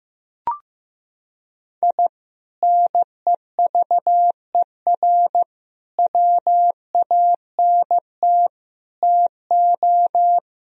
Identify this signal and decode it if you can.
Morse code, heard as a keyed tone, about 15 wpm, 700 Hz. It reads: I NEVER WANT TO
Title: Sentences from Top 1000 Words - Review Part 2 - 15wpm